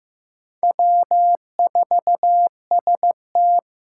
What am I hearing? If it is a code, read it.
Morse code, heard as a keyed tone, about 15 wpm, 700 Hz. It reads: W4ST
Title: Four Challenging Letter-Number Combinations 15wpm